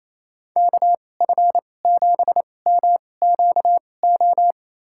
Morse code, heard as a keyed tone, about 28 wpm, spaced as for 20 wpm, 700 Hz. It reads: KF7MQO
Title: USA Call Signs - 6 Characters - 2x Character Spacing - 28wpm